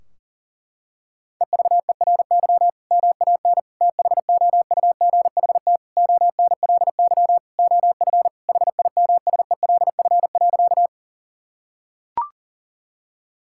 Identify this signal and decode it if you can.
Morse code, heard as a keyed tone, about 40 wpm, 700 Hz. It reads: EVERY MAN THOUGHT ONLY OF HIMSELF.
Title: Sentences from Top 1600 Words 40wpm